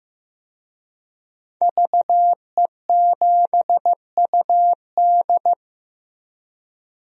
Morse code, heard as a keyed tone, about 15 wpm, 700 Hz. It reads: VE7UD
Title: DXCC-100 Call Signs - 5 Characters - 15wpm